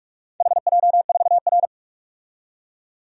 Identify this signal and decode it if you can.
Morse code, heard as a keyed tone, about 48 wpm, 700 Hz. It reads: HJ4R
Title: DXCC-100 Call Signs - 4 Characters - 45wpm